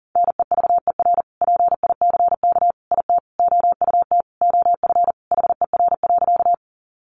Morse code, heard as a keyed tone, about 40 wpm, 700 Hz. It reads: NEVER PICK IT OUT OF HER.
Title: Sentences from Top 1000 Words 40wpm